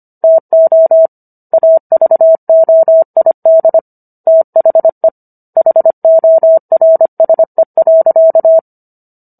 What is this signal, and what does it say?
Morse code, heard as a keyed tone, about 25 wpm, 650 Hz. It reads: TO AVOID THE HORSE.